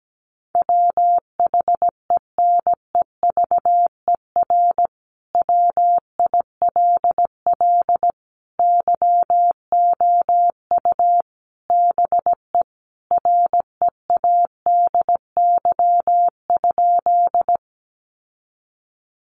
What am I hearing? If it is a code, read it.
Morse code, heard as a keyed tone, about 17 wpm, 700 Hz. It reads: WHENEVER WILL YOU BE READY?
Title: Sentences from Top 1200 Words 17wpm